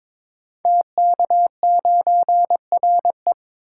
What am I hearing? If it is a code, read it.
Morse code, heard as a keyed tone, about 22 wpm, 700 Hz. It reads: TK9RE
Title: Code Groups - 5 Letter:Numbers - 22wpm